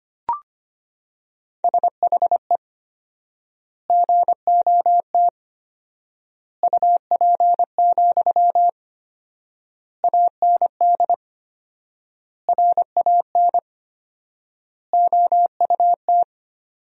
Morse code, heard as a keyed tone, about 25 wpm, 700 Hz. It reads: SHE GOT UP, AND RAN OUT
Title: 4x Word-Spacing - Sentences from Top 500 Words - 25wpm